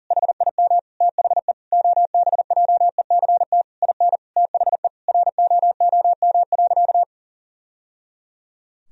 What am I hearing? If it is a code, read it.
Morse code, heard as a keyed tone, about 40 wpm, 700 Hz. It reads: HIM THE OBJECT IN THE ROOM.
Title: Sentences from Top 1000 Words - Review Part 3 - 40wpm